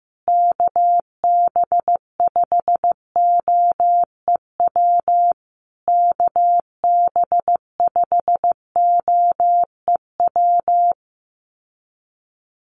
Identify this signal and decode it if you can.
Morse code, heard as a keyed tone, about 15 wpm, 700 Hz. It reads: KB5OEW KB5OEW
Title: USA Call Signs Repeated 3x 15wpm